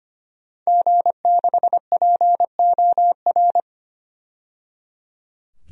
Morse code, heard as a keyed tone, about 25 wpm, 700 Hz. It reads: G6POR
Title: DXCC-100 Call Signs - 5 Characters - 25wpm